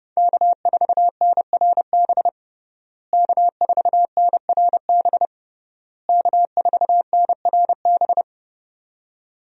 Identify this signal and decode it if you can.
Morse code, heard as a keyed tone, about 30 wpm, 700 Hz. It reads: K4NRB K4NRB K4NRB
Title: USA Call Signs - 5 Characters - Repeated 3x - 30wpm